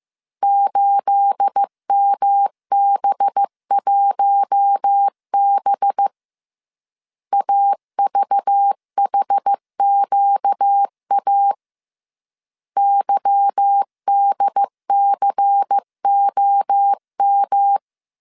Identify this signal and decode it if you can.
Morse code, heard as a keyed tone, about 15 wpm, 800 Hz. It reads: 8MB1B AVHQA YDCOM
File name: CW / DET / 1 9 1500C S A 14wpm 3222